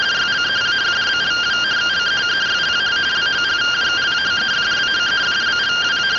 Two-tone teleprinter signal, about 170 Hz shift.
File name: findings / CT 205 ax mips, iDose 3 0005.gen-11 rtty rx